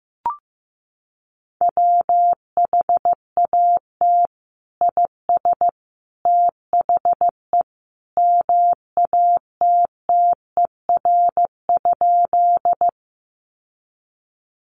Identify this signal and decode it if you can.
Morse code, heard as a keyed tone, about 15 wpm, 700 Hz. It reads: WHAT IS THE MATTER?
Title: Sentences from Top 700 Words 15wpm